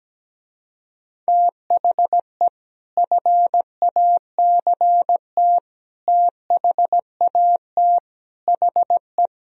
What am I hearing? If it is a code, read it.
Morse code, heard as a keyed tone, about 17 wpm, 700 Hz. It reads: THE FACT THAT HE
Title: Speed-Racing - Sets of 5 Words 17wpm